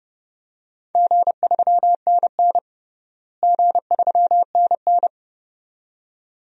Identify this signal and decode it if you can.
Morse code, heard as a keyed tone, about 31 wpm, 700 Hz. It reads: G3NN G3NN
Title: DXCC-100 Call Signs - 4 Characters - Repeated 2x - 30wpm